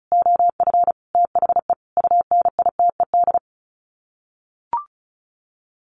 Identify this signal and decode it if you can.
Morse code, heard as a keyed tone, about 34 wpm, 700 Hz. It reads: OF THE UNITED
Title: Sets of 5 Words 35wpm